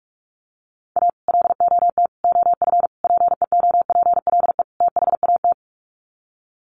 Morse code, heard as a keyed tone, about 44 wpm, 700 Hz. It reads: A LOT OF PEOPLE THAT
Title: Sets of 5 Words Encore2 45wpm